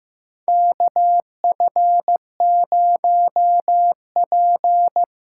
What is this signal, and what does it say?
Morse code, heard as a keyed tone, about 15 wpm, 700 Hz. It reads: KF0P
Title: USA Call Signs - 5 Characters - 6x Character Spacing - 15wpm